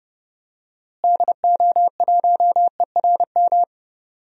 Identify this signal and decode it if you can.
Morse code, heard as a keyed tone, about 32 wpm, 700 Hz. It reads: DO1ERM
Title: DXCC-100 Call Signs - 6 Characters - Repeated 3x - 30wpm